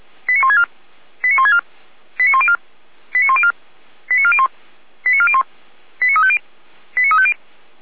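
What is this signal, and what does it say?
Five-tone selective call (ZVEI), heard as quick runs of five tones, tones of about 70 ms. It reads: 89164 89164 89104 89104 89401 89401 89350 89350
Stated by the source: ZVEI five-tone selective call received on SDR.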